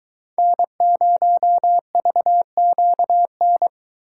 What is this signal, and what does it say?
Morse code, heard as a keyed tone, about 22 wpm, 700 Hz. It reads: N0VQN